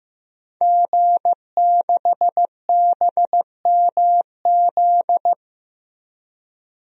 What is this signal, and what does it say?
Morse code, heard as a keyed tone, about 15 wpm, 700 Hz. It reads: G6BMZ
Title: DXCC-100 Call Signs - 5 Characters - Repeated 3x - 15wpm